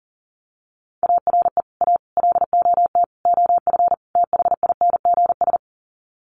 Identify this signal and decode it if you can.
Morse code, heard as a keyed tone, about 40 wpm, 700 Hz. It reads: ARE A LOT OF THINGS
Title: Sets of 5 Words Encore2 40wpm